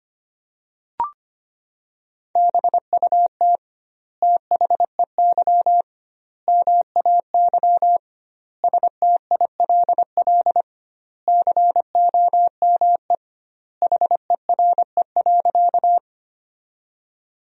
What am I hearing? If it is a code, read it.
Morse code, heard as a keyed tone, about 25 wpm, 700 Hz. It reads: BUT THEY MAY STILL COME HERE.
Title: 2x Word-Spacing - Sentences from Top 300 Words - 25wpm